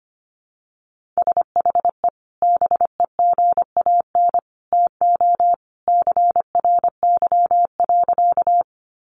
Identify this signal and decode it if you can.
Morse code, heard as a keyed tone, about 25 wpm, 700 Hz. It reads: SHE BEGAN TO CRY.
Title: Sentences from Top 500 Words 25wpm